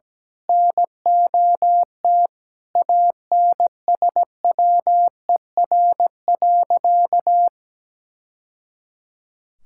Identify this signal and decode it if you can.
Morse code, heard as a keyed tone, about 17 wpm, 700 Hz. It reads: NOT ANSWER.
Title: Sentences from Top 1000 Words - Review Part 6 - 17wpm